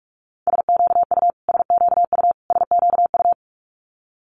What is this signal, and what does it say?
Morse code, heard as a keyed tone, about 46 wpm, 700 Hz. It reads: SQU SQU SQU
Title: Multi-letter Phonemes Repeated 3x 45wpm